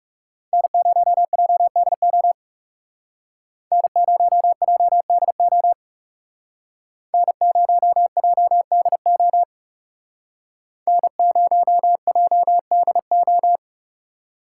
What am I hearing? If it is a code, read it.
Morse code, heard as a keyed tone, about 39 wpm, 700 Hz. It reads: N0JDO N0JDO N0JDO N0JDO